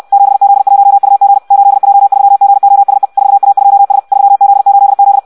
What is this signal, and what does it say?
Morse code, heard as a keyed tone, about 64 wpm, 800 Hz. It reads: 7Z1UG 8J3GOSE 4A2S 3Z8Z
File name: Audio practica CW / qrq-tx-66 2